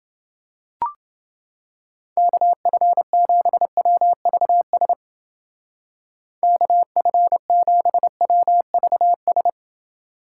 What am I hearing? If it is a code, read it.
Morse code, heard as a keyed tone, about 29 wpm, 700 Hz. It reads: KF7WVS KF7WVS